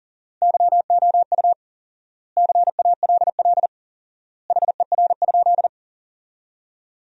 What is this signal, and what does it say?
Morse code, heard as a keyed tone, about 37 wpm, 700 Hz. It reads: YOU CALL HER?
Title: 4x Word-Spacing - Sentences from Top 100 Words - 40wpm